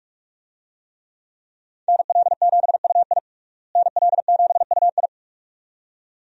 Morse code, heard as a keyed tone, about 46 wpm, 700 Hz. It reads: NL7UI NL7UI
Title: USA Call Signs - 5 Characters - Repeated 2x - 45wpm